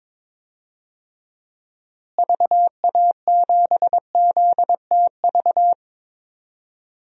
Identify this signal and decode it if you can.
Morse code, heard as a keyed tone, about 22 wpm, 700 Hz. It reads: VA7ZTV
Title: DXCC-100 Call Signs - 6 Characters - Repeated 2x - 22wpm